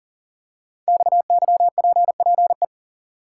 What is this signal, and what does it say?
Morse code, heard as a keyed tone, about 38 wpm, 690 Hz. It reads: XYPPE
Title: Code Groups - 5 Letters - 40wpm